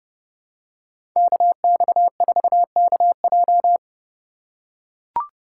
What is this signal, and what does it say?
Morse code, heard as a keyed tone, about 30 wpm, 700 Hz. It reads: KX4KJ
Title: USA Call Signs - 2x Character Spacing - 30wpm